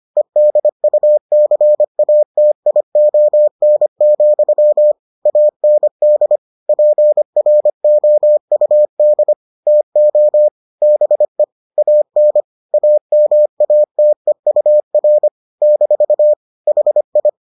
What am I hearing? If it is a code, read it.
Morse code, heard as a keyed tone, about 25 wpm, 600 Hz. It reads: EDUCATION, AND PROUD TO BE AN AMATEUR - HI